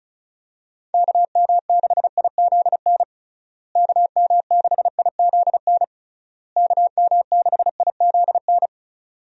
Morse code, heard as a keyed tone, about 34 wpm, 700 Hz. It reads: KM6IZN KM6IZN KM6IZN
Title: USA Call Signs - 6 Characters - Repeated 3x - 35wpm